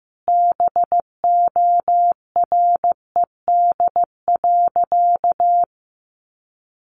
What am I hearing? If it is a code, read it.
Morse code, heard as a keyed tone, about 15 wpm, 700 Hz. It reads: BORED.
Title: Easy Phrases 15wpm